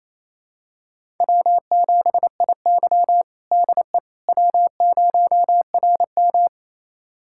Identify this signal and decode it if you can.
Morse code, heard as a keyed tone, about 29 wpm, 700 Hz. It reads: W7IY DE W0RM
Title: USA Callsign DE Callsign - 4 Characters - 28wpm